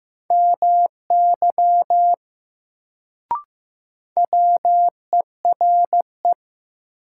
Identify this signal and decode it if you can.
Morse code, heard as a keyed tone, about 15 wpm, 700 Hz. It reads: MY WERE
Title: Sets of 2 Words - New Word Drill - 15wpm